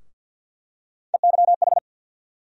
Morse code, heard as a keyed tone, about 50 wpm, 700 Hz. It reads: EYH
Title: USA Call Sign Postfixes - 3 Letters 50wpm